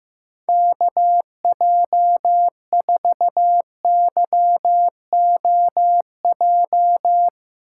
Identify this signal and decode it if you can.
Morse code, heard as a keyed tone, about 15 wpm, 700 Hz. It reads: KJ4YOJ
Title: USA Call Signs - 6 Characters - 4x Character Spacing - 15wpm